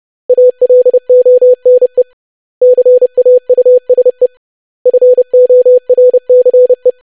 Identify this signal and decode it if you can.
Morse code, heard as a keyed tone, about 30 wpm, 500 Hz. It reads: ALONE CAUSE FORCE
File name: lesson / web / basic500-25wpm